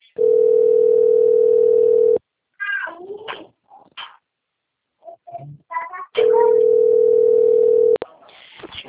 Ringback tone (North America), 440 + 480 Hz.
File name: rg-1600-unknown- - - .16455